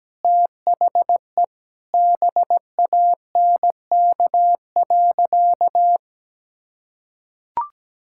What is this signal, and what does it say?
Morse code, heard as a keyed tone, about 17 wpm, 700 Hz. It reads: THE BANK.
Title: Sentences from Top 1000 Words - Review Part 5 - 17wpm